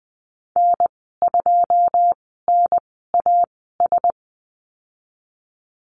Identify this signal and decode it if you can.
Morse code, heard as a keyed tone, about 20 wpm, spaced as for 14 wpm, 700 Hz. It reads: N2NAS
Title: 088 Lesson 15g - Number 5 - Call Signs 20wpm